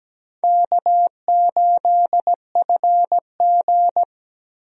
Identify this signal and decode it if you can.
Morse code, heard as a keyed tone, about 17 wpm, 700 Hz. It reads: K8FG
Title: USA Call Signs - 4 Characters - Repeated 3x - 17wpm